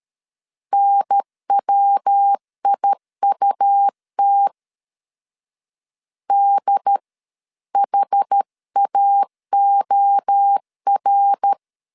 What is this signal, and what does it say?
Morse code, heard as a keyed tone, about 13 wpm, 800 Hz. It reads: NWIUT DHAOR
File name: CW / DET / 1 3 1500C S B 12wpm 2828